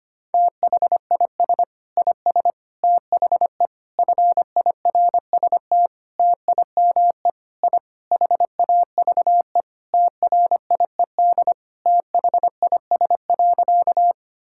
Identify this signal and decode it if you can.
Morse code, heard as a keyed tone, about 25 wpm, 700 Hz. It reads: THIS IS THE FIRST TIME I HAVE TRIED THIS.
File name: practice / 25wpm / Sentences with 4-Word Phrases - 25wpm